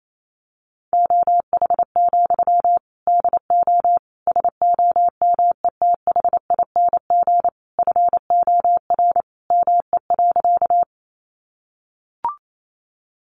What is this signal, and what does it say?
Morse code, heard as a keyed tone, about 28 wpm, 700 Hz. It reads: OH, DO SOMETHING FOR ME.
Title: Sentences from Top 600 Words 28wpm